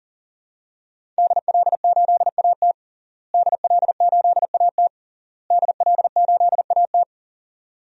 Morse code, heard as a keyed tone, about 40 wpm, 700 Hz. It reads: DL8AT DL8AT DL8AT
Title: DXCC-100 Call Signs - 5 Characters - Repeated 3x - 40wpm